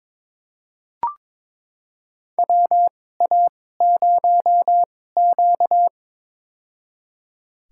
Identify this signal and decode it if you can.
Morse code, heard as a keyed tone, about 21 wpm, spaced as for 16 wpm, 700 Hz. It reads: WA0Q
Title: USA Call Signs - 4 Characters - 2x Character Spacing - 22wpm